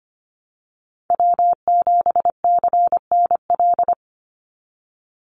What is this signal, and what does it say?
Morse code, heard as a keyed tone, about 25 wpm, 700 Hz. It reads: W7CNL
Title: NA SOTA Chaser Callsigns 25wpm